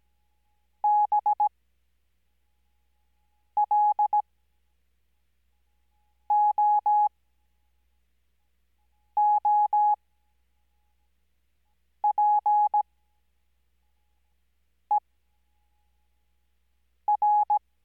Morse code, heard as a keyed tone, about 17 wpm, spaced as for 4 wpm, 840 Hz. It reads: BLOOPER
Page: Farnsworth Spacing: 15WPM Character speed Spaced at 5WPM.